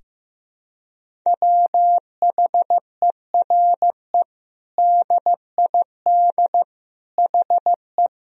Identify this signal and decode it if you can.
Morse code, heard as a keyed tone, about 15 wpm, 700 Hz. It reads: WHERE DID HE
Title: Sentences from Top 200 Words and Abbreviations 15wpm